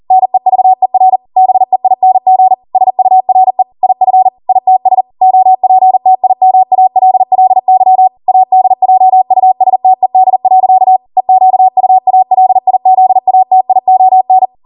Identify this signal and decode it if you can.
Morse code, heard as a keyed tone, about 40 wpm, 750 Hz. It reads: NEVER BEING SURE IF ITS OPTIMALLY ADJUSTED. EQUALIZATION